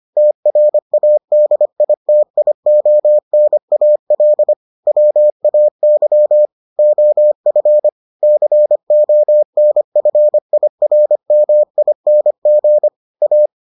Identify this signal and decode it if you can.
Morse code, heard as a keyed tone, about 25 wpm, 600 Hz. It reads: TRADITIONAL WAY OF CONFIRMING A